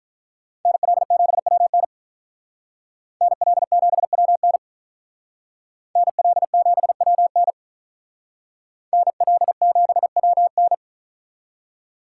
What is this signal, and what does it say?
Morse code, heard as a keyed tone, about 42 wpm, 700 Hz. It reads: NL7WN NL7WN NL7WN NL7WN
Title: Speed Racing - USA 2x2 Call Signs 35wpm